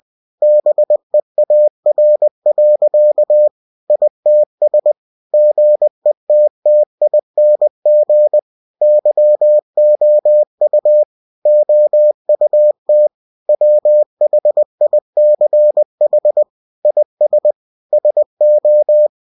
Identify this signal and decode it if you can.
Morse code, heard as a keyed tone, about 20 wpm, 600 Hz. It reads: BEAR. ITS GETTING YOU OUT WHICH IS SO